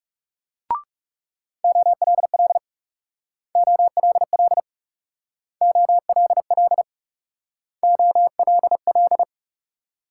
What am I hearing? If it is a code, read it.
Morse code, heard as a keyed tone, about 37 wpm, 700 Hz. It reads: OLL OLL OLL OLL
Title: Speed-Racing Multi-letter Phonemes - 30wpm